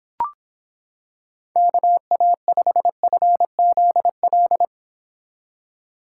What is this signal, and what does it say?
Morse code, heard as a keyed tone, about 26 wpm, 700 Hz. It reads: KA5FZL